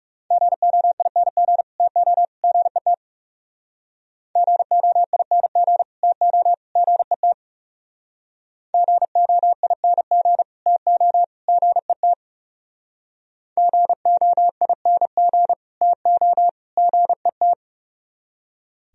Morse code, heard as a keyed tone, about 38 wpm, 700 Hz. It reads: GOING TO GET GOING TO GET GOING TO GET GOING TO GET